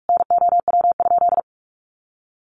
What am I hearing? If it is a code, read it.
Morse code, heard as a keyed tone, about 44 wpm, 700 Hz. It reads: NOW?
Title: Sentences from Top 900 Words 45wpm